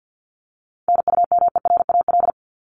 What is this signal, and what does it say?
Morse code, heard as a keyed tone, about 50 wpm, 700 Hz. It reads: NUMERAL
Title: Top 500 Words 50wpm